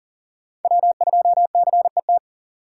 Morse code, heard as a keyed tone, about 37 wpm, 700 Hz. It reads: W2CET
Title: USA Call Signs - 5 Characters - 2x Character Spacing - 40wpm